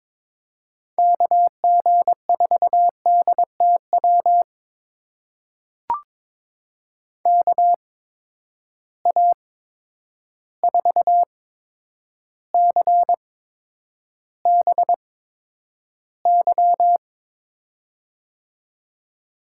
Morse code, heard as a keyed tone, about 21 wpm, spaced as for 9 wpm, 700 Hz. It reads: KG4DTW KA4CBY
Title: USA Call Signs - 6 Characters - 8x Character Spacing - 22wpm